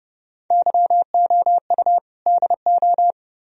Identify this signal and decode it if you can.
Morse code, heard as a keyed tone, about 29 wpm, 700 Hz. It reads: YOU DO